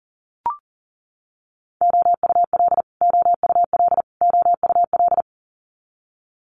Morse code, heard as a keyed tone, about 38 wpm, 700 Hz. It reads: OUL OUL OUL
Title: Multi-letter Phonemes Repeated 3x 40wpm